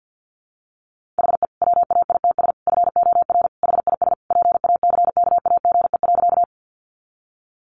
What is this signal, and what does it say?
Morse code, heard as a keyed tone, about 50 wpm, 700 Hz. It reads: HE WAITS FOR HIS PACKAGE.
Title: Additional Easy Phrases 50wpm